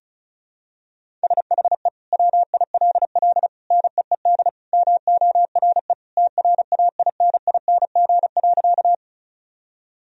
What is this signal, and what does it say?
Morse code, heard as a keyed tone, about 36 wpm, 700 Hz. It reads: SHE WILL NEED MORE TRAINING.
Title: Sentences from Top 1000 Words - Review Part 4 - 35wpm